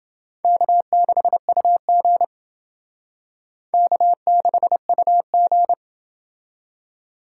Morse code, heard as a keyed tone, about 28 wpm, 700 Hz. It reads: K6UG K6UG